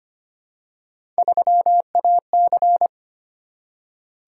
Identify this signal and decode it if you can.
Morse code, heard as a keyed tone, about 25 wpm, 700 Hz. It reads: 3AC
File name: Three Letter-Number Combinations - 25wpm